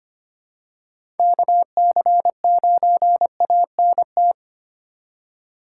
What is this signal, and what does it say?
Morse code, heard as a keyed tone, about 25 wpm, 700 Hz. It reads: KC9ANT
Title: USA Call Signs - 4x Character Spacing - 25wpm